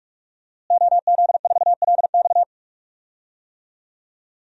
Morse code, heard as a keyed tone, about 46 wpm, 700 Hz. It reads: OZ4LX
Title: DXCC-100 Call Signs - 5 Characters - Repeated 2x - 45wpm